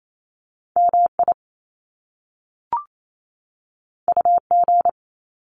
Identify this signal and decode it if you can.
Morse code, heard as a keyed tone, about 28 wpm, 700 Hz. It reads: MI UG
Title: Top Two Letter Combinations in Words 28wpm